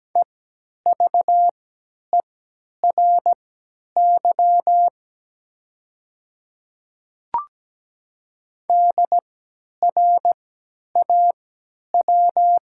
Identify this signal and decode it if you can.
Morse code, heard as a keyed tone, about 17 wpm, spaced as for 10 wpm, 700 Hz. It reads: EVERY DRAW
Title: 3x Character Spacing - Top 300 Words - 17wpm